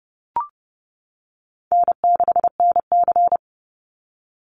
Morse code, heard as a keyed tone, about 30 wpm, 700 Hz. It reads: N6NC
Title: Easy Call Signs - 30wpm